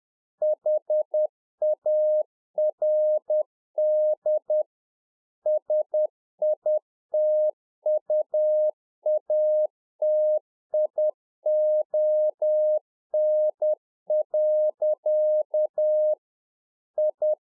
Morse code, heard as a keyed tone, about 10 wpm, 610 Hz. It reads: HARD SITUATION. I
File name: UNFOGETTABLE QSO 10WPM